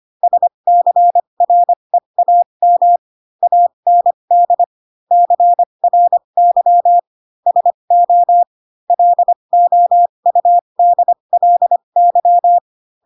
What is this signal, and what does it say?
Morse code, heard as a keyed tone, about 25 wpm, spaced as for 22 wpm, 700 Hz. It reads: SCREAM AND CRY SO LOUDLY
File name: Pinocchio / Pinocchio ch7 18 25 ik1hge